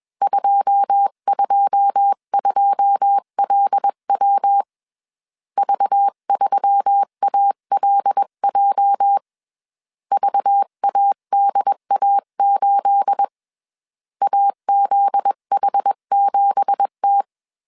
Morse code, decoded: 222LW V3ALJ VADA8 AZH7T

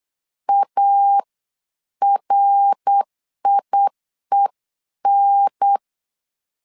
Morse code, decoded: ARIEN

